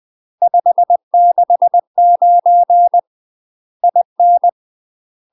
Morse code, heard as a keyed tone, about 20 wpm, 700 Hz. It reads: 569 IN
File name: UK MASTER QSOs TRAINING / 20WPM / UK MASTER QSOs 0036